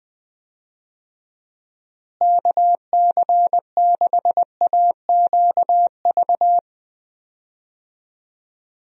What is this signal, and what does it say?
Morse code, heard as a keyed tone, about 20 wpm, 700 Hz. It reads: KC6AQV
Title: USA Call Signs - 6 Characters - Repeated 2x - 20wpm